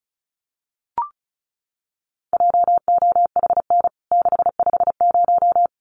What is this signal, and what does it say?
Morse code, heard as a keyed tone, about 34 wpm, 700 Hz. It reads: JOHN 650
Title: CWT Exchange 35wpm